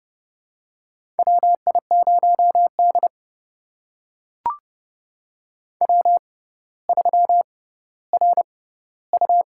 Morse code, heard as a keyed tone, about 29 wpm, spaced as for 15 wpm, 700 Hz. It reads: WI0D W3RU
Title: USA Call Signs - 4 Characters - 6x Character Spacing - 30wpm